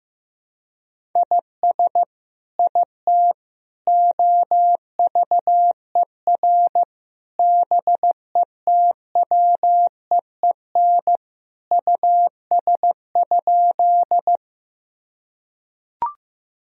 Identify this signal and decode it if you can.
Morse code, heard as a keyed tone, about 15 wpm, 700 Hz. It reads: IS IT OVER BETWEEN US?
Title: Sentences from Top 1000 Words - Review Part 1 - 15wpm